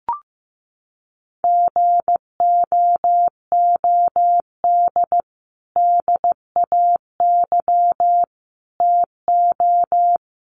Morse code, heard as a keyed tone, about 15 wpm, 700 Hz. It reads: GOOD DAY TO
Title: Sentences from Top 200 Words 15wpm